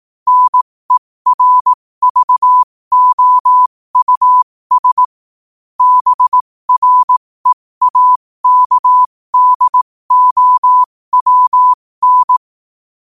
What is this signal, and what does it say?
Morse code, heard as a keyed tone, about 18 wpm, spaced as for 15 wpm, 1000 Hz. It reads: NERVOUS BREAKDOWN
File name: qotdfiles / 15wpm